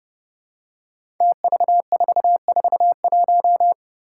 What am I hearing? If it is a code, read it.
Morse code, heard as a keyed tone, about 31 wpm, 700 Hz. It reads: TV441
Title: Five Challenging Letter-Number Combinations 30wpm